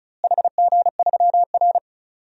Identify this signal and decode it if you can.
Morse code, heard as a keyed tone, about 34 wpm, 700 Hz. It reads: HG3R
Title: DXCC-100 Call Signs - 4 Characters - Repeated 2x - 35wpm